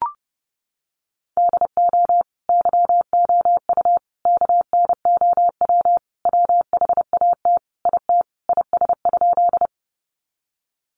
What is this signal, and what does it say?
Morse code, heard as a keyed tone, about 30 wpm, 700 Hz. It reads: DO YOU KNOW WHAT IT IS?
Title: Sentences from Top 100 Words 30wpm